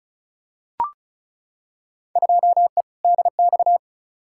Morse code, heard as a keyed tone, about 34 wpm, 700 Hz. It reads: 2E DX
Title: Field Day Exchanges - 35wpm